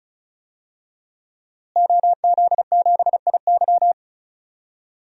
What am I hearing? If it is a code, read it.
Morse code, heard as a keyed tone, about 36 wpm, 700 Hz. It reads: OZ7IY
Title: DXCC-100 Call Signs - 5 Characters - Repeated 3x - 35wpm